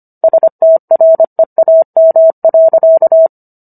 Morse code, heard as a keyed tone, about 25 wpm, 650 Hz. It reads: STREAM.